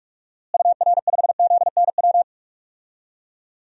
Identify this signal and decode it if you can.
Morse code, heard as a keyed tone, about 46 wpm, 700 Hz. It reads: UR5ZNW
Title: DXCC-100 Call Signs - 6 Characters - Repeated 3x - 45wpm